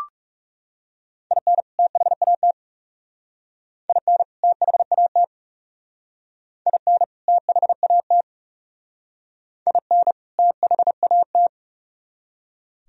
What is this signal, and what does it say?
Morse code, heard as a keyed tone, about 36 wpm, 700 Hz. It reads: IN THAT IN THAT IN THAT IN THAT
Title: Speed-Racing - Sets of 2 Words 30wpm